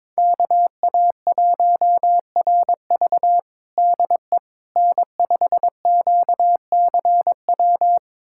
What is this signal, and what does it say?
Morse code, heard as a keyed tone, about 22 wpm, 700 Hz. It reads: KA1RV DE N5QCW
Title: USA Callsign DE Callsign - 5 Characters - 22wpm